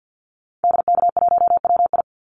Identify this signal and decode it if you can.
Morse code, heard as a keyed tone, about 50 wpm, 700 Hz. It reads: DK1WI
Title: Call Signs Repeated 2x - 50wpm